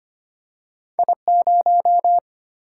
Morse code, heard as a keyed tone, about 25 wpm, 700 Hz. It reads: I0
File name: Two Letter-Number Combinations - 25wpm